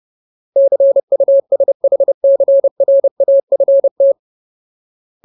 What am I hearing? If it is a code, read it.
Morse code, heard as a keyed tone, about 30 wpm, 550 Hz. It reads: CUSHCRAFT
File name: Ultimate Word List / 30WPM Ultimate